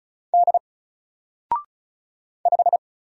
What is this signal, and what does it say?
Morse code, heard as a keyed tone, about 33 wpm, 700 Hz. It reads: D 5
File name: Single-Letter-Number-35wpm